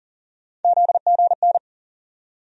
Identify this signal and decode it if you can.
Morse code, heard as a keyed tone, about 40 wpm, 700 Hz. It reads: ZGN
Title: USA Call Sign Postfixes - 3 Letters 40wpm